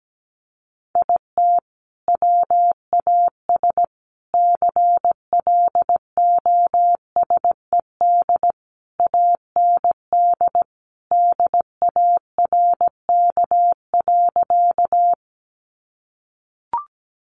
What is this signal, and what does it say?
Morse code, heard as a keyed tone, about 17 wpm, 700 Hz. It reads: IT WAS CLOSED AND DARK.
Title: Sentences from Top 1400 Words 17wpm